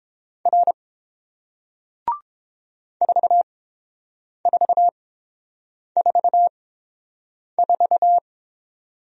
Morse code, heard as a keyed tone, about 30 wpm, spaced as for 7 wpm, 700 Hz. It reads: R 4444